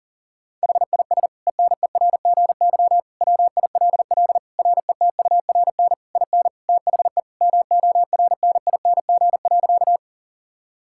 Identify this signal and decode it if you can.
Morse code, heard as a keyed tone, about 40 wpm, 700 Hz. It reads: HIS ENERGY WILL RETURN IN THE MORNING.
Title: Sentences from Top 1000 Words - Review Part 6 - 40wpm